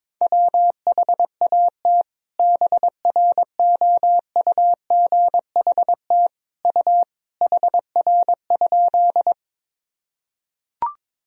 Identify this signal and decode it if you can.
Morse code, heard as a keyed tone, about 21 wpm, 700 Hz. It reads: WHAT BROUGHT U HR?
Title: Sentences from Top 500 Words and Abbreviations 22wpm